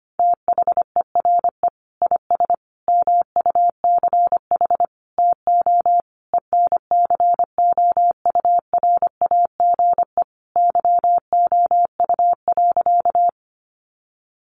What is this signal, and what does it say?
Morse code, heard as a keyed tone, about 25 wpm, 700 Hz. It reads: THERE IS MUCH TO ENCOURAGE YOU.
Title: Sentences from Top 1400 Words 25wpm